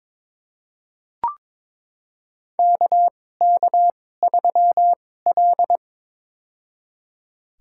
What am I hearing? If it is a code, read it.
Morse code, heard as a keyed tone, about 22 wpm, spaced as for 16 wpm, 700 Hz. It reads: KK3L
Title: USA Call Signs - 4 Characters - 2x Character Spacing - 22wpm